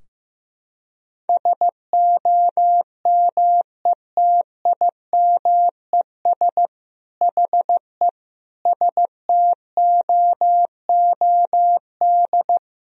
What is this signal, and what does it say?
Morse code, heard as a keyed tone, about 15 wpm, 700 Hz. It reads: SOMETIMES HE STOOD
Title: Sentences from Top 1000 Words - Review Part 4 - 15wpm